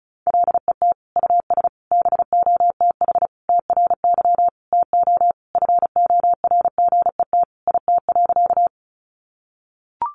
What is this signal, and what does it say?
Morse code, heard as a keyed tone, about 34 wpm, 700 Hz. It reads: LET US BOTH TRY TO FORGET IT.
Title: Sentences from Top 1200 Words 35wpm